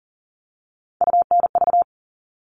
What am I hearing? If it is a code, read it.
Morse code, heard as a keyed tone, about 40 wpm, 700 Hz. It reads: UNV
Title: All Cut Numbers - 3-Digit 40wpm